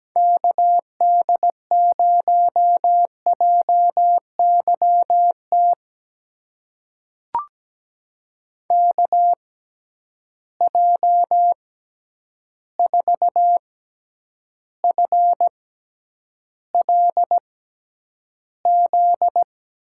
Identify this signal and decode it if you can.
Morse code, heard as a keyed tone, about 17 wpm, spaced as for 9 wpm, 700 Hz. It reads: KD0JYT KJ4FLZ
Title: USA Call Signs - 6 Characters - 6x Character Spacing - 17wpm